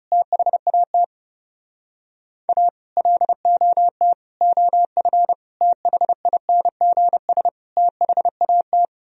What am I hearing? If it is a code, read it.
Morse code, heard as a keyed tone, about 37 wpm, 700 Hz. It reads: THAT A LOT OF THINGS THAT
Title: Speed-Racing - Sets of 5 Words 30wpm